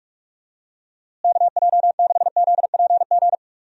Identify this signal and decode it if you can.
Morse code, heard as a keyed tone, about 46 wpm, 700 Hz. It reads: KJ6ZPG
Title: USA Call Signs - 6 Characters - Repeated 2x - 45wpm